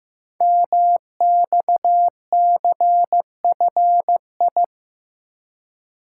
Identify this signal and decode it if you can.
Morse code, heard as a keyed tone, about 15 wpm, 700 Hz. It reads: MXCFI
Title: Code Groups - 5 Letters - 15wpm